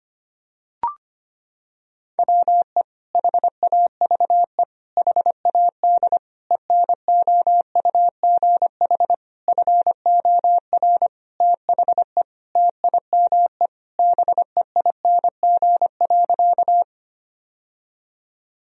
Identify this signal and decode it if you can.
Morse code, heard as a keyed tone, about 25 wpm, 700 Hz. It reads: WE HAVE HAD ENOUGH FOR THE TIME BEING.
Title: Sentences from Top 1000 Words - Review Part 3 - 25wpm